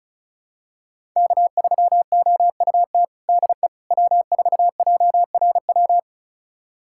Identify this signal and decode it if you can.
Morse code, heard as a keyed tone, about 36 wpm, 700 Hz. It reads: K3OUT DE W4JRW
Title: USA Callsign DE Callsign - 5 Characters - 35wpm